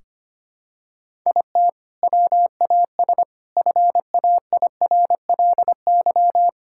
Morse code, heard as a keyed tone, about 25 wpm, 700 Hz. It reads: IT WAS FAIRLY
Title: Sentences from Top 1600 Words 25wpm